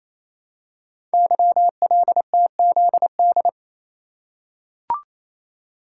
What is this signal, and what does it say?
Morse code, heard as a keyed tone, about 27 wpm, 700 Hz. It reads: YLTZD
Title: Code Groups - 5 Letter:Numbers - 28wpm